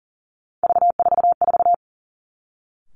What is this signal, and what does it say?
Morse code, heard as a keyed tone, about 38 wpm, 700 Hz. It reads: V44
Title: Three Challenging Letter-Number Combinations 40wpm